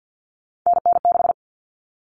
Morse code, heard as a keyed tone, about 50 wpm, 700 Hz. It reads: NN6
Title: Common Cut Numbers - 3-Digit 50wpm